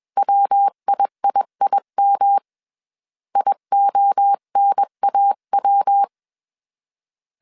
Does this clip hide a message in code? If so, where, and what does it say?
Yes: Morse code from 0.2 to 6.0 s, heard as a keyed tone, about 21 wpm, 800 Hz. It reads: WIIIM IONAW